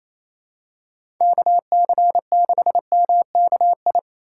Morse code, heard as a keyed tone, about 28 wpm, 700 Hz. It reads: KC6MKI
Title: USA Call Signs - 6 Characters - 2x Character Spacing - 28wpm